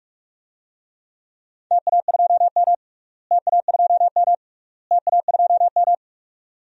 Morse code, heard as a keyed tone, about 46 wpm, 700 Hz. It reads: TA2M TA2M TA2M
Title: DXCC-100 Call Signs - 4 Characters - Repeated 3x - 45wpm